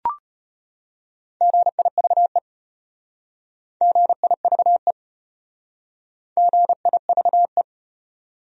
Morse code, heard as a keyed tone, about 32 wpm, 700 Hz. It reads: GIVE GIVE GIVE